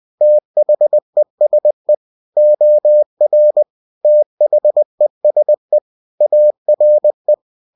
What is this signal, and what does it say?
Morse code, heard as a keyed tone, about 20 wpm, 600 Hz. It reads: THESE OR THESE ARE
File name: cw-books / winnie-the-pooh / winnie-20wpm-page-0068